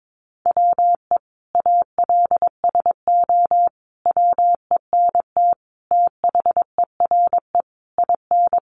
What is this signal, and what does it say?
Morse code, heard as a keyed tone, about 22 wpm, 700 Hz. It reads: WE ALSO WENT THERE IN
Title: Sentences from Top 1200 Words 22wpm